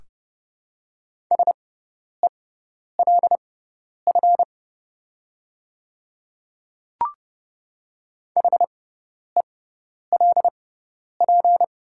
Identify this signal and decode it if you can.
Morse code, heard as a keyed tone, about 29 wpm, spaced as for 10 wpm, 690 Hz. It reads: SELF HELP
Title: 6x Character Spacing - Top 200 Words - 30wpm